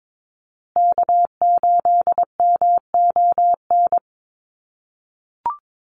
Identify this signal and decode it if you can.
Morse code, heard as a keyed tone, about 22 wpm, 700 Hz. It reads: K8MON
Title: USA Call Signs Repeated 2x 22wpm